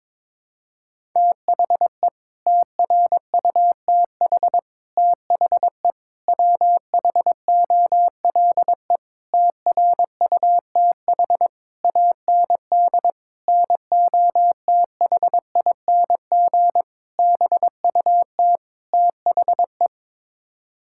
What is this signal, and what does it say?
Morse code, heard as a keyed tone, about 22 wpm, 700 Hz. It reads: THE TRUTH THE WHOLE TRUTH AND NOTHING BUT THE
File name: Sets of 9 Words - 22wpm